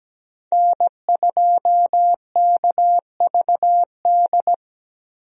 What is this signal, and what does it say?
Morse code, heard as a keyed tone, about 17 wpm, 700 Hz. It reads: N2KVD
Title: USA Call Signs - 5 Characters - Repeated 2x - 17wpm